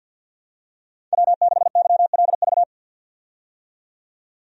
Morse code, heard as a keyed tone, about 50 wpm, 700 Hz. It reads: W6YLV
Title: USA Call Signs - 5 Characters - 50wpm